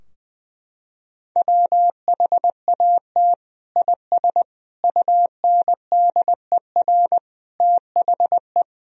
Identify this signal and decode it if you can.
Morse code, heard as a keyed tone, about 21 wpm, 700 Hz. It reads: WHAT IS UNDER THE
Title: Sentences from Top 1000 Words - Review Part 2 - 20wpm